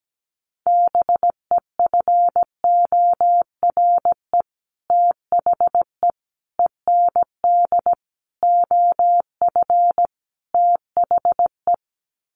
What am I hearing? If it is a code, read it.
Morse code, heard as a keyed tone, about 17 wpm, 700 Hz. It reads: BEFORE THE END OF THE
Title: Sets of 5 Words Encore 17wpm